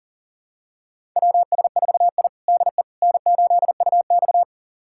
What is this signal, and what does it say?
Morse code, heard as a keyed tone, about 40 wpm, 700 Hz. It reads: WS4I DE N8UX
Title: USA Callsign DE Callsign - 4 Characters - 40wpm